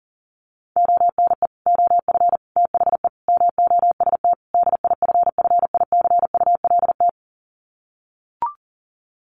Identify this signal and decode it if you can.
Morse code, heard as a keyed tone, about 40 wpm, 700 Hz. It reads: ONE OF THE MOST DIFFICULT
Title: Sets of 5 Words Encore2 40wpm